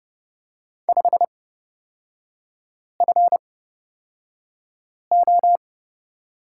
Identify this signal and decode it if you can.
Morse code, heard as a keyed tone, about 30 wpm, spaced as for 5 wpm, 700 Hz. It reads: 5FO
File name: course / 30:10wpm / 236 Lesson 38e - Letter q - Three Characters 30wpm